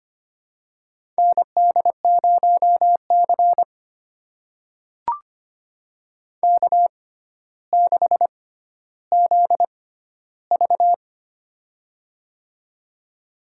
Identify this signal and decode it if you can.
Morse code, heard as a keyed tone, about 25 wpm, spaced as for 13 wpm, 700 Hz. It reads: ND0C K6ZV
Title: USA Call Signs - 4 Characters - 6x Character Spacing - 25wpm